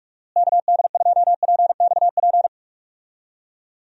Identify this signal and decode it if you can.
Morse code, heard as a keyed tone, about 44 wpm, 700 Hz. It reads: KD2PXP
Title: USA Call Signs - 6 Characters - Repeated 3x - 45wpm